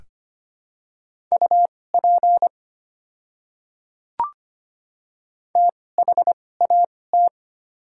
Morse code, heard as a keyed tone, about 25 wpm, spaced as for 18 wpm, 690 Hz. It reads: UP THAT
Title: 2x Character Spacing - Top 100 Words - 25wpm